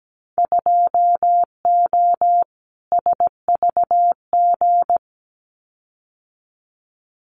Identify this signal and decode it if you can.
Morse code, heard as a keyed tone, about 17 wpm, 700 Hz. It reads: 2O SVG
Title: Winter Field Day Exchanges 17wpm